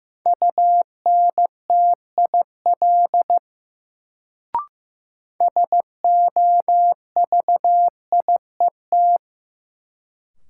Morse code, heard as a keyed tone, about 15 wpm, 700 Hz. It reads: UNTIL SOVIET
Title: Sets of 5 Words Encore - New Word Drill - 15wpm